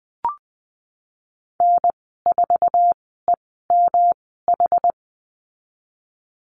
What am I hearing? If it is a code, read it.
Morse code, heard as a keyed tone, about 20 wpm, spaced as for 15 wpm, 700 Hz. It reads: N4EMH
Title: 186 Lesson 30g - Letter b - Call Signs 20wpm